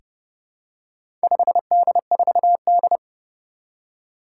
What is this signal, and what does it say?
Morse code, heard as a keyed tone, about 31 wpm, 700 Hz. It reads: 5D4D
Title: Four Challenging Letter-Number Combinations 30wpm